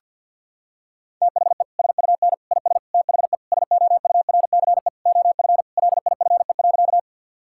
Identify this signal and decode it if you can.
Morse code, heard as a keyed tone, about 52 wpm, 700 Hz. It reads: THE SUN IS THE SOURCE OF LIFE.
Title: Sentences from Top 1000 Words - Review Part 5 - 50wpm